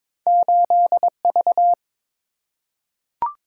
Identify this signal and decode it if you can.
Morse code, heard as a keyed tone, about 22 wpm, 700 Hz. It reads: 8V
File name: practice / 22wpm / Two Letter-Number Combinations - 22wpm